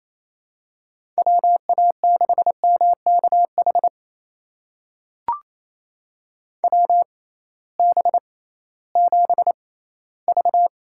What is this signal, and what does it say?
Morse code, heard as a keyed tone, about 28 wpm, spaced as for 16 wpm, 700 Hz. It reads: WA6MKH WB7V
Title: USA Call Signs - 6 Characters - 6x Character Spacing - 28wpm